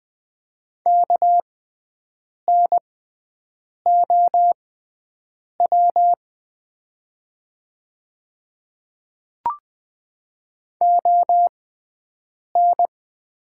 Morse code, heard as a keyed tone, about 20 wpm, spaced as for 7 wpm, 690 Hz. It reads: KNOW ON
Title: 6x Character Spacing - Top 100 Words - 20wpm